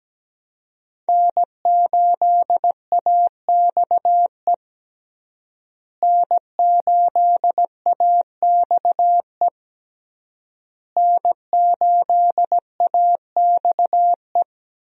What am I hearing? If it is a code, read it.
Morse code, heard as a keyed tone, about 17 wpm, 700 Hz. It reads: N8AXE N8AXE N8AXE
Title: USA Call Signs - 5 Characters - Repeated 3x - 17wpm